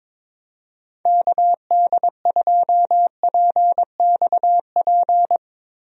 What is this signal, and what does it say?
Morse code, heard as a keyed tone, about 22 wpm, 700 Hz. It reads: KD2PXP
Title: USA Call Signs - 6 Characters - Repeated 3x - 22wpm